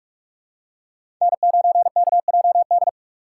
Morse code, heard as a keyed tone, about 42 wpm, 690 Hz. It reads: N9KJD
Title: USA Call Signs - 5 Characters - 6x Character Spacing - 45wpm